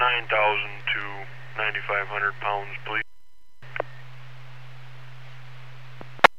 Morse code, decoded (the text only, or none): none